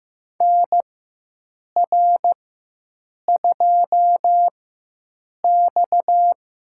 Morse code, heard as a keyed tone, about 15 wpm, spaced as for 7 wpm, 700 Hz. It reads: NR2X
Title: USA Call Signs - 4 Characters - 4x Character Spacing - 15wpm